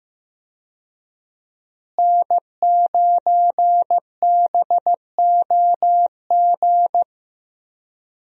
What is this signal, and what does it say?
Morse code, heard as a keyed tone, about 15 wpm, 700 Hz. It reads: N9BOG
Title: USA Call Signs - 5 Characters - Repeated 3x - 15wpm